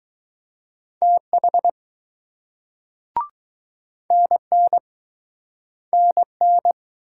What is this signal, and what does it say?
Morse code, heard as a keyed tone, about 20 wpm, 700 Hz. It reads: TH NN NN